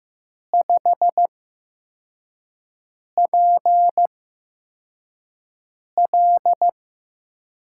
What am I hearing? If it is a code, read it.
Morse code, heard as a keyed tone, about 15 wpm, spaced as for 4 wpm, 700 Hz. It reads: 5PL
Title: USA Call Signs - 4 Characters - 8x Character Spacing - 15wpm